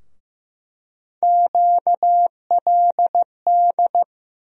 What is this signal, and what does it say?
Morse code, heard as a keyed tone, about 15 wpm, 700 Hz. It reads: QLD
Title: USA Call Sign Postfixes - 3 Letters 15wpm